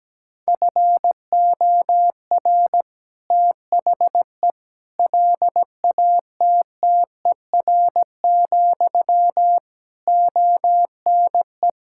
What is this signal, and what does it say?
Morse code, heard as a keyed tone, about 17 wpm, 700 Hz. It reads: FOR THE LATTER, ONE
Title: Sentences from Top 1600 Words 17wpm